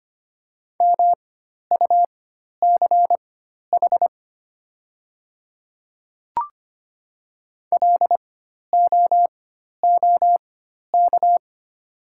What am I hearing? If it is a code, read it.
Morse code, heard as a keyed tone, about 25 wpm, spaced as for 12 wpm, 700 Hz. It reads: MUCH LOOK
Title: 4x Character Spacing - Top 300 Words - 25wpm